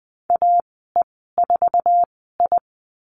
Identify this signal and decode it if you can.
Morse code, heard as a keyed tone, about 20 wpm, spaced as for 15 wpm, 700 Hz. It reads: AE4I
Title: 055 Lesson 10g - Letter r - Call Signs 20wpm